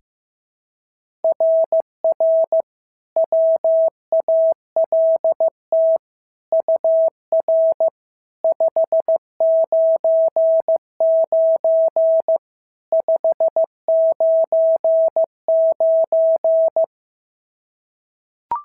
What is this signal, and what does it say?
Morse code, heard as a keyed tone, about 14 wpm, 650 Hz. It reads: RR WALT UR 599 599
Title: SOTA QSO Practice 15wpm